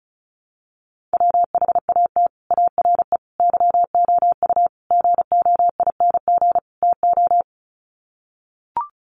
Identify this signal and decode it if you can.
Morse code, heard as a keyed tone, about 34 wpm, 700 Hz. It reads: WHAT ARE YOU GOING TO
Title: Sets of 5 Words 35wpm